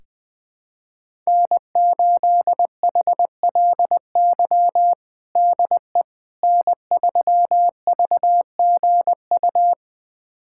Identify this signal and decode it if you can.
Morse code, heard as a keyed tone, about 20 wpm, 700 Hz. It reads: N8HLY DE N3VGU
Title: USA Callsign DE Callsign - 5 Characters - 20wpm